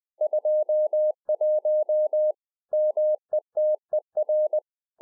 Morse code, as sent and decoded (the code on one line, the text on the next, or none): ..--- .---- / -- . - . .-.
21 METER